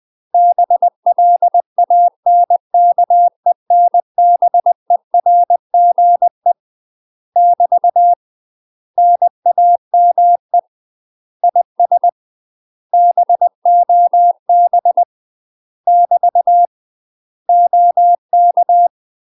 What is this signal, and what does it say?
Morse code, heard as a keyed tone, about 20 wpm, 700 Hz. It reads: BLANKENBERGE <BT> NAME IS BOB <BT> OK
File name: UK MASTER QSOs TRAINING / 20WPM / UK MASTER QSOs 0070